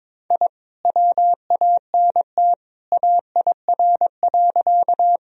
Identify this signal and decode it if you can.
Morse code, heard as a keyed tone, about 22 wpm, 700 Hz. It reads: I WANT AIR.
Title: Sentences from Top 1000 Words - Review Part 1 - 22wpm